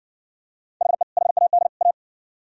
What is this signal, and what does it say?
Morse code, heard as a keyed tone, about 60 wpm, 700 Hz. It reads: HE HAD A